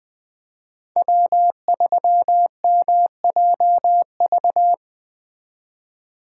Morse code, heard as a keyed tone, about 20 wpm, 700 Hz. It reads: W3MJV
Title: Speed Racing - USA 1x3 Call Signs 17wpm